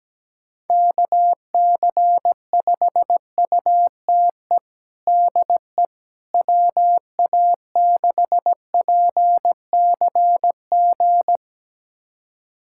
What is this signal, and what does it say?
Morse code, heard as a keyed tone, about 17 wpm, 700 Hz. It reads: KC5UTE DE WA6PCG
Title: USA Callsign DE Callsign 17wpm